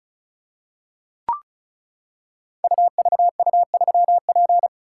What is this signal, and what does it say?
Morse code, heard as a keyed tone, about 36 wpm, 700 Hz. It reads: UVU3P
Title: Code Groups - 5 Letter:Numbers - 35wpm